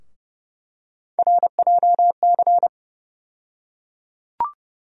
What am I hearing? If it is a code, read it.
Morse code, heard as a keyed tone, about 30 wpm, 700 Hz. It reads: RJC
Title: USA Call Sign Postfixes - 3 Letters 30wpm